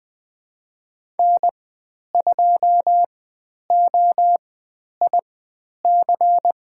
Morse code, heard as a keyed tone, about 20 wpm, spaced as for 10 wpm, 700 Hz. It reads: N2OIC
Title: 102 Lesson 17g - Letter u - Call Signs 20wpm